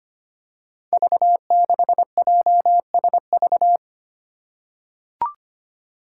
Morse code, heard as a keyed tone, about 25 wpm, 700 Hz. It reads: V6JSV
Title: Five Challenging Letter-Number Combinations 25wpm